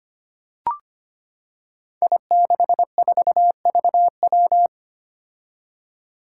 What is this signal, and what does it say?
Morse code, heard as a keyed tone, about 25 wpm, 700 Hz. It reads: I64VW
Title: Five Challenging Letter-Number Combinations 25wpm